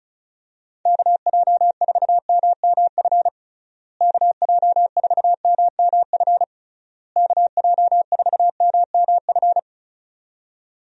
Morse code, heard as a keyed tone, about 36 wpm, 700 Hz. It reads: KJ4MMF KJ4MMF KJ4MMF
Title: USA Call Signs - 6 Characters - Repeated 3x - 35wpm